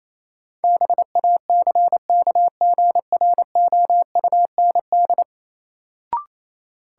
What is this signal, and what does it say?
Morse code, heard as a keyed tone, about 28 wpm, 700 Hz. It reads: BACKGROUND
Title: Sentences from Top 1300 Words - New Word Drill - 28wpm